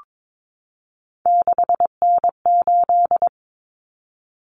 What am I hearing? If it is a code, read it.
Morse code, heard as a keyed tone, about 22 wpm, 700 Hz. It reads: 6N8
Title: Common Cut Numbers - 3-Digit 22wpm